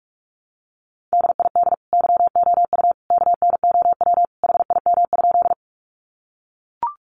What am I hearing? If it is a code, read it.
Morse code, heard as a keyed tone, about 44 wpm, 700 Hz. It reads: DID YOU KNOW HIM?
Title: Sentences from Top 100 Words 45wpm